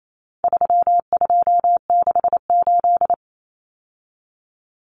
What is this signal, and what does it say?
Morse code, heard as a keyed tone, about 28 wpm, 700 Hz. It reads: 3268
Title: Numbers - 4-Digit 28wpm